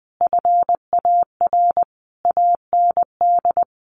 Morse code, heard as a keyed tone, about 20 wpm, 700 Hz. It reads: FAR AND
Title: Sentences from Top 1200 Words 20wpm